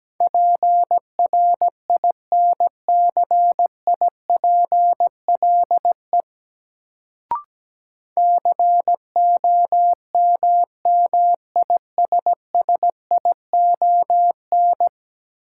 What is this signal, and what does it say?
Morse code, heard as a keyed tone, about 17 wpm, 700 Hz. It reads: PRINCIPLE COMMISSION
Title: Sentences from Top 1400 Words - New Word Drill - 17wpm